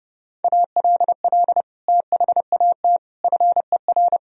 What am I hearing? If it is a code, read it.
Morse code, heard as a keyed tone, about 30 wpm, 700 Hz. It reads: ALL THAT FER
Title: Sentences from Top 500 Words and Abbreviations 30wpm